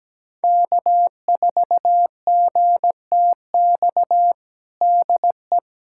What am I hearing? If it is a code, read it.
Morse code, heard as a keyed tone, about 17 wpm, 700 Hz. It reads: K4GTX DE
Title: USA Callsign DE Callsign 17wpm